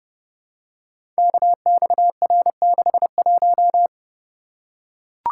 Morse code, heard as a keyed tone, about 30 wpm, 700 Hz. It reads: KXR61
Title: Code Groups - 5 Letter:Numbers - 30wpm